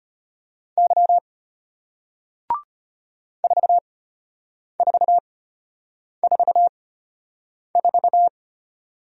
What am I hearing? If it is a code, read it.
Morse code, heard as a keyed tone, about 34 wpm, spaced as for 8 wpm, 700 Hz. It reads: Y 4444